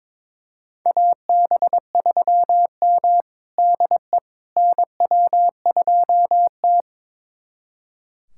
Morse code, heard as a keyed tone, about 22 wpm, 700 Hz. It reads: AB3M DE NW2T
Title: USA Callsign DE Callsign - 4 Characters - 22wpm